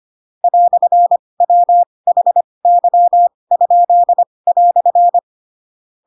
Morse code, heard as a keyed tone, about 25 wpm, spaced as for 20 wpm, 700 Hz. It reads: "WHY?"
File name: Pinocchio / Pinocchio ch4 15 25 ik1hge